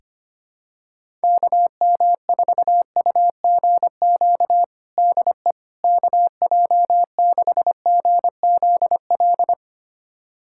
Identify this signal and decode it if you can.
Morse code, heard as a keyed tone, about 25 wpm, 700 Hz. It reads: KM4UGQ DE KJ6GZL
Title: USA Callsign DE Callsign 25wpm